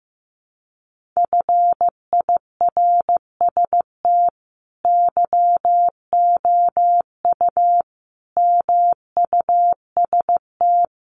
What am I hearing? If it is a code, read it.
Morse code, heard as a keyed tone, about 15 wpm, 700 Hz. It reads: FIRST YOU MUST
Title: Sentences from Top 1500 Words 15wpm